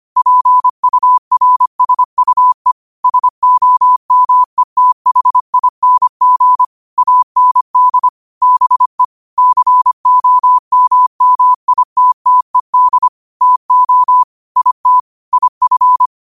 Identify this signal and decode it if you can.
Morse code, heard as a keyed tone, about 25 wpm, 1000 Hz. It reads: PURSUE SOMETHING AND BE COMMITTED TO IT IF
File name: qotdfiles / 25wpm